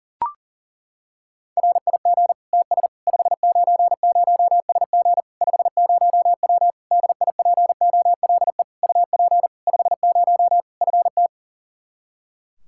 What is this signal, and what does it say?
Morse code, heard as a keyed tone, about 40 wpm, 700 Hz. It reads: RIG TS 590SG 50W DIPOLE UP 50 FT
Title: Practice QSO - Exchange 2 - Info - 40wpm